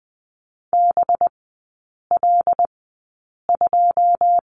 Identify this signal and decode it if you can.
Morse code, heard as a keyed tone, about 20 wpm, spaced as for 8 wpm, 700 Hz. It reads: BL2
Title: 184 Lesson 30e - Letter b - Three Characters 20wpm